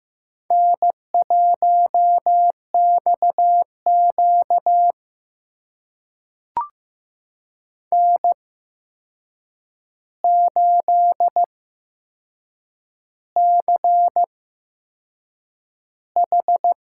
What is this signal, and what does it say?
Morse code, heard as a keyed tone, about 15 wpm, spaced as for 6 wpm, 700 Hz. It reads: N1XQ N8CH
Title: USA Call Signs - 4 Characters - 8x Character Spacing - 15wpm